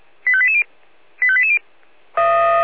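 Five-tone selective call (ZVEI), heard as quick runs of five tones, tones of about 70 ms. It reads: 85880 85880
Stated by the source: ZVEI five-tone selective call 85859 received by SDR.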